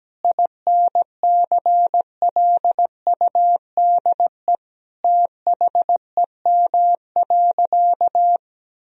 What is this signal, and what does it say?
Morse code, heard as a keyed tone, about 17 wpm, 700 Hz. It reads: INCLUDE THEM.
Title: Sentences from Top 1000 Words - Review Part 3 - 17wpm